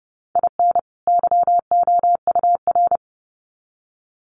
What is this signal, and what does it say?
Morse code, heard as a keyed tone, about 30 wpm, 700 Hz. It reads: IN YOUR
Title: Sets of 2 Words 30wpm